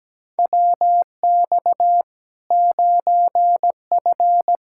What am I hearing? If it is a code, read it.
Morse code, heard as a keyed tone, about 17 wpm, 700 Hz. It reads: WX 9F
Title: Practice QSO - Exchange 2 - Info - 17wpm